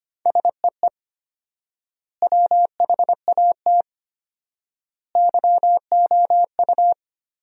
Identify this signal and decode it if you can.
Morse code, heard as a keyed tone, about 25 wpm, 700 Hz. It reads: SEE WHAT YOU
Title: 4x Word-Spacing - Sentences from Top 500 Words - 25wpm